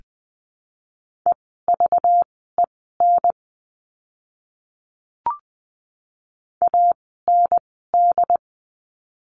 Morse code, heard as a keyed tone, about 19 wpm, spaced as for 14 wpm, 700 Hz. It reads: EVEN AND
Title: 230 Lesson 37f - Letter x - Words 20wpm